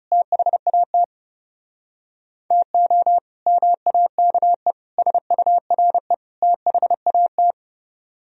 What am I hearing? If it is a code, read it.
Morse code, heard as a keyed tone, about 40 wpm, 700 Hz. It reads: THAT TO MAKE SURE THAT